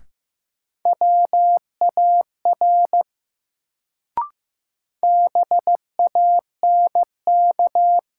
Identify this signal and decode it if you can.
Morse code, heard as a keyed tone, about 15 wpm, 700 Hz. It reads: WAR BANK
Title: Sets of 5 Words - New Word Drill - 15wpm